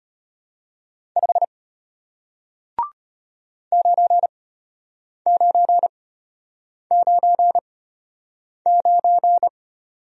Morse code, heard as a keyed tone, about 33 wpm, spaced as for 8 wpm, 700 Hz. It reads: 5 9999